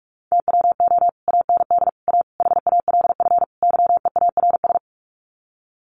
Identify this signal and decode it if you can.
Morse code, heard as a keyed tone, about 44 wpm, 700 Hz. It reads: TWO AND A HALF YEARS
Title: Sets of 5 Words 45wpm